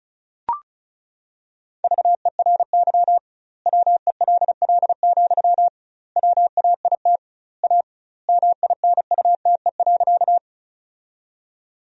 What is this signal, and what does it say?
Morse code, heard as a keyed tone, about 36 wpm, 700 Hz. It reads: VERY WELL, WAIT A MINUTE.
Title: 2x Word-Spacing - Sentences from Top 500 Words - 35wpm